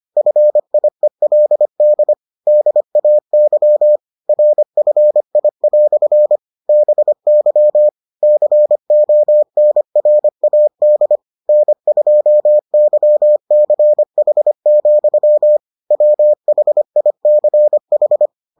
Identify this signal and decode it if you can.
Morse code, heard as a keyed tone, about 25 wpm, 600 Hz. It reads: FIELD DAY RFI" BY CONRAD N2YCH, WHICH